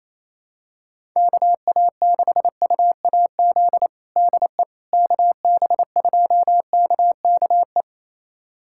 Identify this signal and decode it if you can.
Morse code, heard as a keyed tone, about 28 wpm, 700 Hz. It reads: KA6UAZ DE KB2KKE
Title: USA Callsign DE Callsign - 6 Characters - 28wpm